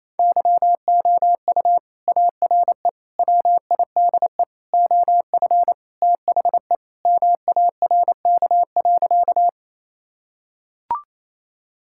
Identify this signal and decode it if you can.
Morse code, heard as a keyed tone, about 27 wpm, 700 Hz. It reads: YOU ARE WIDE OF THE MARK.
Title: Sentences from Top 1000 Words - Review Part 6 - 28wpm